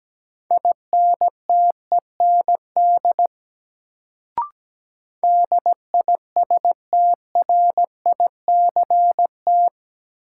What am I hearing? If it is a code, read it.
Morse code, heard as a keyed tone, about 17 wpm, 700 Hz. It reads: INTEND DISTRICT
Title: Sentences from Top 1600 Words - New Word Drill - 17wpm